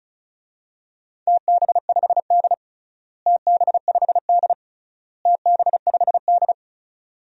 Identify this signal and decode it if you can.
Morse code, heard as a keyed tone, about 35 wpm, 700 Hz. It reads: TB5D TB5D TB5D
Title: DXCC-100 Call Signs - 4 Characters - Repeated 3x - 35wpm